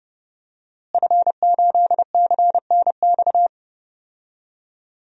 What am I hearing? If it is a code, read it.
Morse code, heard as a keyed tone, about 30 wpm, 700 Hz. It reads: F8CNX
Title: DXCC-100 Call Signs - 5 Characters - Repeated 2x - 30wpm